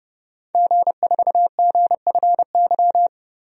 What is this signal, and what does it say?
Morse code, heard as a keyed tone, about 31 wpm, 700 Hz. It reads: G4GFY
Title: DXCC-100 Call Signs - 5 Characters - 30wpm